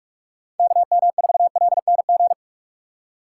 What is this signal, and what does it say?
Morse code, heard as a keyed tone, about 45 wpm, 700 Hz. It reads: KM4LNG
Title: USA Call Signs - 6 Characters - 6x Character Spacing - 45wpm